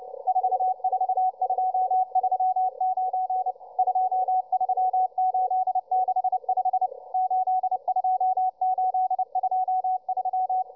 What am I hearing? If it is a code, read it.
Morse code, heard as a keyed tone, about 29 wpm, 750 Hz, with radio noise. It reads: 44239 23865 82823